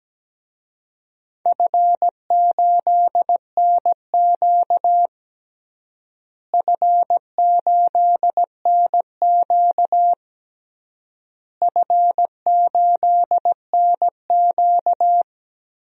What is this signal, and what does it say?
Morse code, heard as a keyed tone, about 17 wpm, 700 Hz. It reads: F8NQ F8NQ F8NQ
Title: DXCC-100 Call Signs - 4 Characters - Repeated 3x - 17wpm